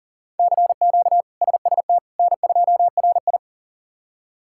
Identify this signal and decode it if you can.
Morse code, heard as a keyed tone, about 40 wpm, 700 Hz. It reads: CQ SST N2RI
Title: K1USN SST QSO Practice - 40wpm